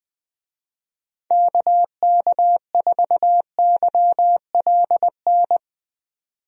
Morse code, heard as a keyed tone, about 21 wpm, 700 Hz. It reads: KK4YLN
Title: USA Call Signs - 6 Characters - 20wpm